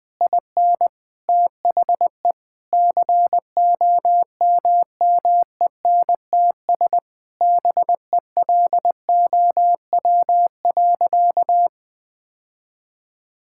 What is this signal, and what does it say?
Morse code, heard as a keyed tone, about 20 wpm, 700 Hz. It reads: IN THE COMMENTS BELOW.
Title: Sentences with 9-Word Phrases - 20wpm